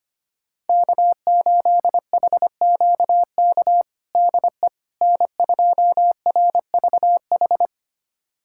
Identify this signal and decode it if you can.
Morse code, heard as a keyed tone, about 25 wpm, 700 Hz. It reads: K8HQK DE N2RVH
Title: USA Callsign DE Callsign - 5 Characters - 25wpm